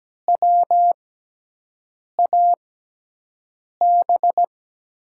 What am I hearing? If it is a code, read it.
Morse code, heard as a keyed tone, about 17 wpm, spaced as for 6 wpm, 700 Hz. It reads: WAB
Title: USA Call Signs - 6x Character Spacing - 17wpm